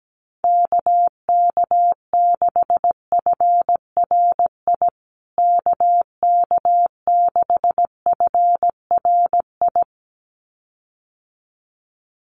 Morse code, heard as a keyed tone, about 17 wpm, 700 Hz. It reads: KK6FRI KK6FRI
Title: USA Call Signs Repeated 2x 17wpm